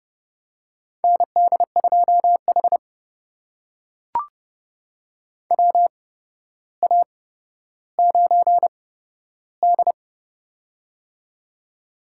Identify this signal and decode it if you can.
Morse code, heard as a keyed tone, about 29 wpm, spaced as for 13 wpm, 700 Hz. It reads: ND2H WA9D
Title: USA Call Signs - 4 Characters - 8x Character Spacing - 30wpm